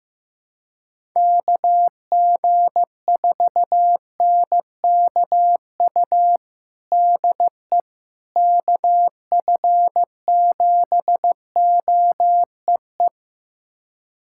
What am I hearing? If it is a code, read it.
Morse code, heard as a keyed tone, about 15 wpm, 700 Hz. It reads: KG4NKU DE KF7OEE
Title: USA Callsign DE Callsign - 6 Characters - 15wpm